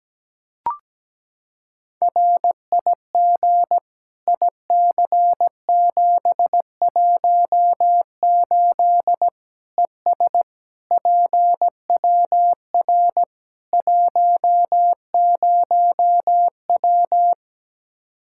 Morse code, heard as a keyed tone, about 17 wpm, 700 Hz. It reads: RIG IC718 ES PWR 10W
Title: Practice QSO - Exchange 2 - Info - 17wpm